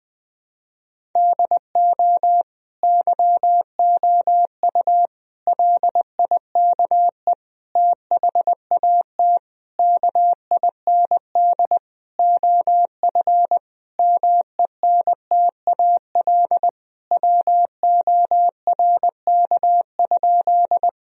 Morse code, heard as a keyed tone, about 20 wpm, 700 Hz. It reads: DO YOU LIKE THAT KIND OF MENTAL WORK?
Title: Sentences from Top 1000 Words - Review Part 4 - 20wpm